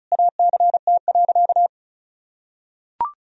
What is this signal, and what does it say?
Morse code, heard as a keyed tone, about 36 wpm, 700 Hz. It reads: ACT.
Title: Sentences from Top 1000 Words - Review Part 1 - 35wpm